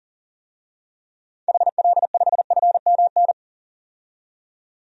Morse code, heard as a keyed tone, about 37 wpm, 700 Hz. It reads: HL5FMN
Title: DXCC-100 Call Signs - 6 Characters - 40wpm